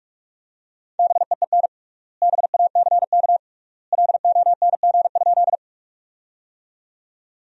Morse code, heard as a keyed tone, about 46 wpm, 700 Hz. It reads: BEEN BACK LONG?
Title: 3x Word-Spacing - Sentences from Top 200 Words - 45wpm